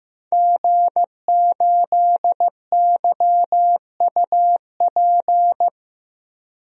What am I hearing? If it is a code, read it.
Morse code, heard as a keyed tone, about 15 wpm, 700 Hz. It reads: G8YUP
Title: DXCC-100 Call Signs - 5 Characters - Repeated 2x - 15wpm